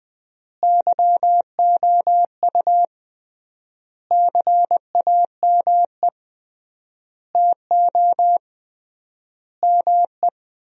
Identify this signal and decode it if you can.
Morse code, heard as a keyed tone, about 20 wpm, 700 Hz. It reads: YOU CAME TO ME
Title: 3x Word-Spacing - Sentences from Top 200 Words - 20wpm